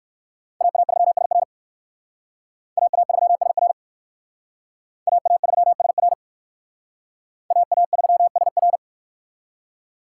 Morse code, heard as a keyed tone, about 57 wpm, 700 Hz. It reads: AA3SR AA3SR AA3SR AA3SR